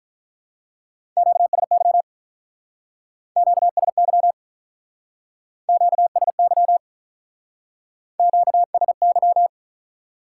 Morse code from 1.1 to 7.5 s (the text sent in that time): QSY QSY QSY